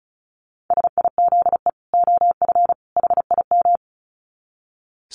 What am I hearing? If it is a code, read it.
Morse code, heard as a keyed tone, about 34 wpm, 700 Hz. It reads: SIZE OF HIM
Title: Dit Challenge 35wpm